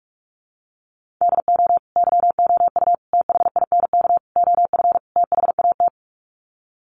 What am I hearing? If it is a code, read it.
Morse code, heard as a keyed tone, about 44 wpm, 700 Hz. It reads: DO YOU THINK OF THAT
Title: Sets of 5 Words Encore2 45wpm